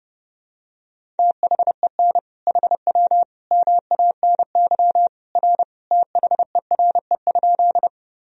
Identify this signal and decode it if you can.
Morse code, heard as a keyed tone, about 30 wpm, 700 Hz. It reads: THEN HW MANY R THERE?
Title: Sentences from Top 100 Words and Abbreviations 30wpm